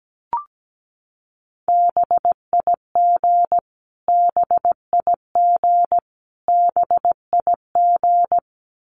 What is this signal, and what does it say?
Morse code, heard as a keyed tone, about 17 wpm, 700 Hz. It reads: BIG BIG BIG
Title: Top 500 Words Repeated 17wpm